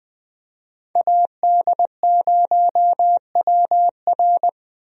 Morse code, heard as a keyed tone, about 20 wpm, 700 Hz. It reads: AD0WR
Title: USA Call Signs - 5 Characters - Repeated 3x - 20wpm